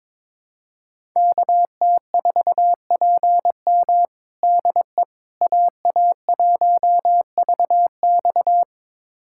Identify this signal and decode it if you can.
Morse code, heard as a keyed tone, about 22 wpm, 700 Hz. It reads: KT4PM DE AA1VX
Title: USA Callsign DE Callsign - 5 Characters - 22wpm